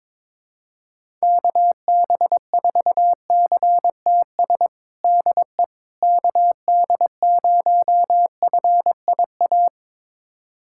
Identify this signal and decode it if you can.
Morse code, heard as a keyed tone, about 21 wpm, 700 Hz. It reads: KB4CTS DE KD0FIA
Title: USA Callsign DE Callsign 22wpm